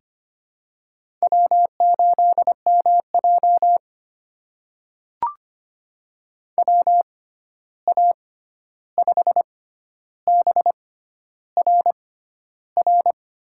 Morse code, heard as a keyed tone, about 25 wpm, spaced as for 11 wpm, 700 Hz. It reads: W8MJ WA5BRR
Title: USA Call Signs - 6x Character Spacing - 25wpm